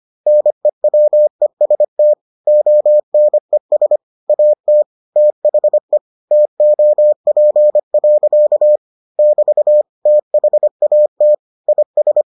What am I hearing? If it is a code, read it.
Morse code, heard as a keyed tone, about 25 wpm, 600 Hz. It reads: NEWEST ONES AT THE TOP. <BT> THAT IS